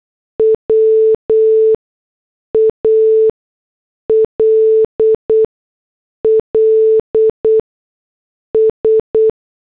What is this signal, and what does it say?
Morse code, heard as a keyed tone, about 8 wpm, spaced as for 6 wpm, 440 Hz. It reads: WALLS